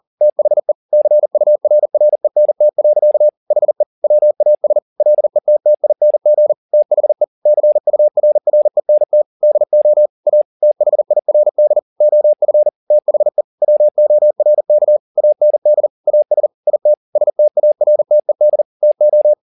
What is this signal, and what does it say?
Morse code, heard as a keyed tone, about 40 wpm, 600 Hz. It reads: THE CURRENT. HE WAS LETTING THE CURRENT DO A THIRD OF THE WORK AND AS IT STARTED TO